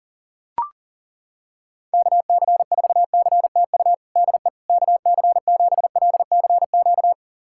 Morse code, heard as a keyed tone, about 40 wpm, 690 Hz. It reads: KC4CTU DE KC7LCQ
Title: USA Callsign DE Callsign - 6 Characters - 40wpm